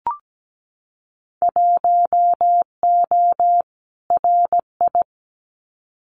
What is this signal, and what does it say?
Morse code, heard as a keyed tone, about 17 wpm, 700 Hz. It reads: 1O RI
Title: Winter Field Day Exchanges 17wpm